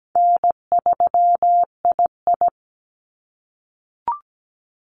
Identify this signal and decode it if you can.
Morse code, heard as a keyed tone, about 17 wpm, 700 Hz. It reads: N3II
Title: NA SOTA Activator Callsigns 17wpm